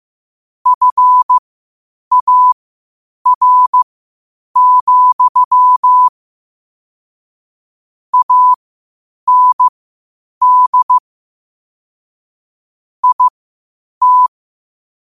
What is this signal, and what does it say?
Morse code, heard as a keyed tone, about 15 wpm, spaced as for 8 wpm, 1000 Hz. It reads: FAR, AND IT
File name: qotdfiles / 8wpm